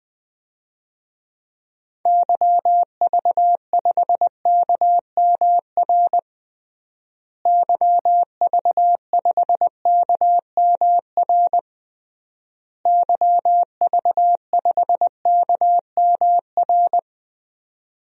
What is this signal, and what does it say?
Morse code, heard as a keyed tone, about 20 wpm, 700 Hz. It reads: YV5KMR YV5KMR YV5KMR
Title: DXCC-100 Call Signs - 6 Characters - Repeated 3x - 20wpm